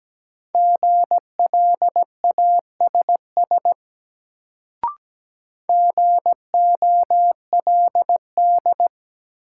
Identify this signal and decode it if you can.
Morse code, heard as a keyed tone, about 17 wpm, 700 Hz. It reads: GLASS GOLD
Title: Sentences from Top 1000 Words - Review Part 3 - New Word Drill - 17wpm